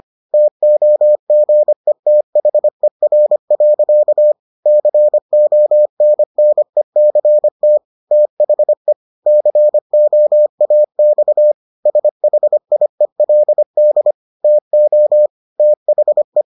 Morse code, heard as a keyed tone, about 25 wpm, 600 Hz. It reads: TOGETHER. CONNECT THE COAX SHIELD TO THE